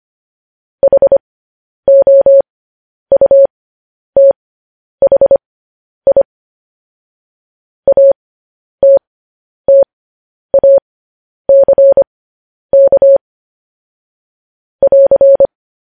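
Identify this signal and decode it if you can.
Morse code, heard as a keyed tone, about 25 wpm, spaced as for 10 wpm, 550 Hz. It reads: HOUTHI ATTACK <AR>
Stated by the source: Morse news headlines practice at 10.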